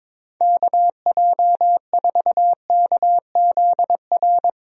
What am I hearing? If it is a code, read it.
Morse code, heard as a keyed tone, about 22 wpm, 700 Hz. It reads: KJ4KZR
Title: USA Call Signs - 6 Characters - 22wpm